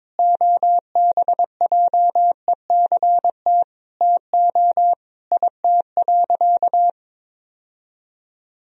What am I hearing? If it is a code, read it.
Morse code, heard as a keyed tone, about 22 wpm, 700 Hz. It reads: OBJECT TO IT.
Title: Sentences from Top 1000 Words - Review Part 3 - 22wpm